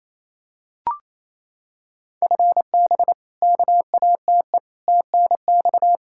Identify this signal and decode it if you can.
Morse code, heard as a keyed tone, about 28 wpm, 700 Hz. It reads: FB KATE TNX
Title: Practice QSO - Exchange 3 - More Info - 28wpm